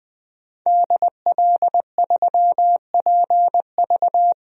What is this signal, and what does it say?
Morse code, heard as a keyed tone, about 20 wpm, 700 Hz. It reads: DL3PV
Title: DXCC-100 Call Signs - 5 Characters - Repeated 2x - 20wpm